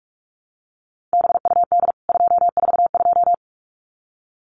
Morse code, heard as a keyed tone, about 44 wpm, 700 Hz. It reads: BUD 242
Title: CWT Exchange 45wpm